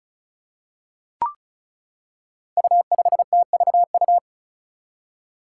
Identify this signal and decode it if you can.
Morse code, heard as a keyed tone, about 34 wpm, 700 Hz. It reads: U5TVU
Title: Five Challenging Letter-Number Combinations 35wpm